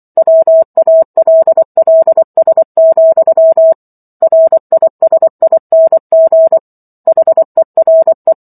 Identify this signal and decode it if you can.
Morse code, heard as a keyed tone, about 24 wpm, 650 Hz. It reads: WALLS, RISING HERE